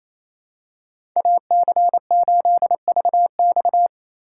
Morse code, heard as a keyed tone, about 28 wpm, 700 Hz. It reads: AC8VX
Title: USA Call Signs - 5 Characters - 2x Character Spacing - 28wpm